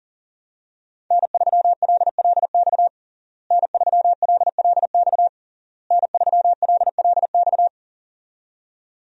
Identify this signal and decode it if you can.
Morse code, heard as a keyed tone, about 40 wpm, 700 Hz. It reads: N3LLX N3LLX N3LLX
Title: USA Call Signs - 5 Characters - Repeated 3x - 40wpm